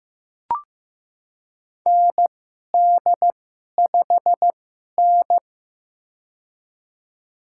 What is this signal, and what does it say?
Morse code, heard as a keyed tone, about 15 wpm, spaced as for 11 wpm, 700 Hz. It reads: ND5N
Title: USA Call Signs - 2x Character Spacing - 15wpm